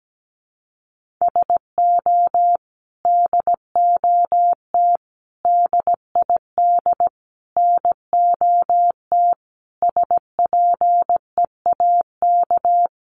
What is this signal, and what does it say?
Morse code, heard as a keyed tone, about 17 wpm, 700 Hz. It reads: SO DOT DID NOT SPEAK
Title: Sentences from Top 1500 Words 17wpm